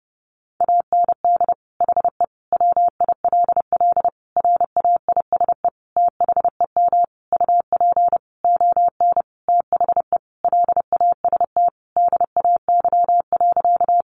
Morse code, heard as a keyed tone, about 30 wpm, 700 Hz. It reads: AND HE WILL RAISE THEM UP ON THE LAST DAY.
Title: Sentences from Top 1400 Words 30wpm